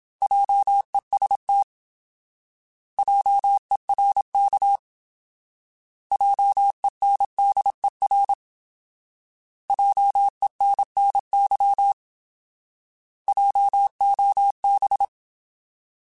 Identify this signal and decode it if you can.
Morse code, heard as a keyed tone, about 27 wpm, 800 Hz. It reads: JEST JERK JENDER JENNY JOB
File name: Alphabetical / Alphabetical mp3s / Js @25